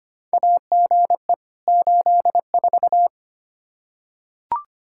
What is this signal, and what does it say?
Morse code, heard as a keyed tone, about 25 wpm, 700 Hz. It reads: AGE 84
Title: Practice QSO - Exchange 3 - More Info - 25wpm